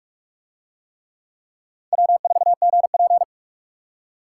Morse code, heard as a keyed tone, about 39 wpm, 700 Hz. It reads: W4GP